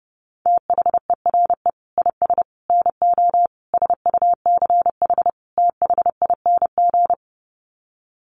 Morse code, heard as a keyed tone, about 30 wpm, 700 Hz. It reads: THERE IS NO SUCH THING
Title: Sets of 5 Words Encore 30wpm